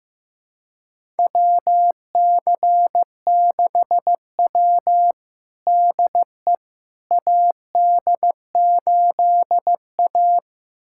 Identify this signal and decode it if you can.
Morse code, heard as a keyed tone, about 15 wpm, 700 Hz. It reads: WC6W DE AD8A
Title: USA Callsign DE Callsign - 4 Characters - 15wpm